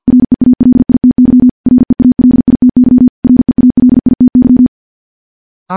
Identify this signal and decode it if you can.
Morse code, heard as a keyed tone, about 51 wpm, 250 Hz. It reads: REALITY REALITY REALITY
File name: morse / morse-training-50WPM